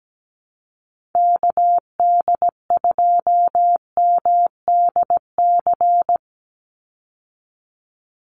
Morse code, heard as a keyed tone, about 17 wpm, 700 Hz. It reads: KD2MDC
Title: USA Call Signs 17wpm